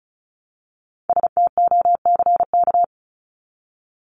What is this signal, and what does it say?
Morse code, heard as a keyed tone, about 34 wpm, 700 Hz. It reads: STOCK
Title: Top 501-750 Words 35wpm